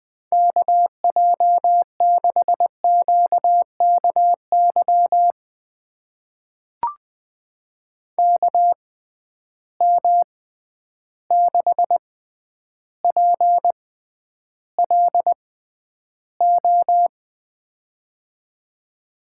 Morse code, heard as a keyed tone, about 20 wpm, spaced as for 10 wpm, 700 Hz. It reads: KJ6QKY KM6PLO
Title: USA Call Signs - 6 Characters - 6x Character Spacing - 20wpm